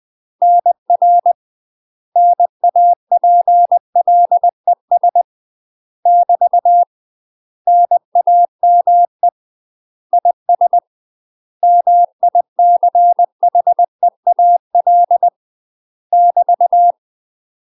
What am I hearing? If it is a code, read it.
Morse code, heard as a keyed tone, about 20 wpm, 700 Hz. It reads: NR NAPLES <BT> NAME IS MICHEAL <BT>
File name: UK MASTER QSOs TRAINING / 20WPM / UK MASTER QSOs 0059